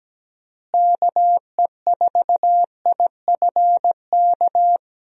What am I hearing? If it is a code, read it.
Morse code, heard as a keyed tone, about 17 wpm, 700 Hz. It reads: KE4IFK
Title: USA Call Signs - 6 Characters - Repeated 3x - 17wpm